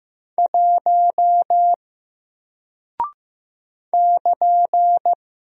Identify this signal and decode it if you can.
Morse code, heard as a keyed tone, about 15 wpm, 700 Hz. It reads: 1 <KN>
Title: Single Character (Letter:Number:Punctuation:Prosign) 15wpm